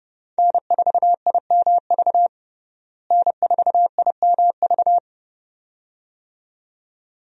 Morse code, heard as a keyed tone, about 32 wpm, 700 Hz. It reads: N4IMV N4IMV
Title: USA Call Signs - 5 Characters - Repeated 2x - 30wpm